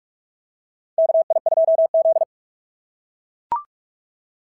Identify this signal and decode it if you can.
Morse code, heard as a keyed tone, about 41 wpm, 690 Hz. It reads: KI2Z
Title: SOTA QSO Practice 45wpm